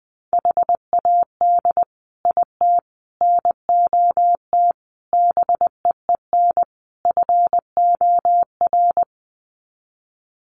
Morse code, heard as a keyed tone, about 20 wpm, 700 Hz. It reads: HAD IT NOT BEEN FOR
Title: Sets of 5 Words Encore2 20wpm